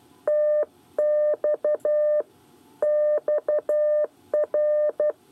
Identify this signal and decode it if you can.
Morse code, heard as a keyed tone, about 12 wpm, 600 Hz. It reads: TX XR
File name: 8 bis lezione cw 1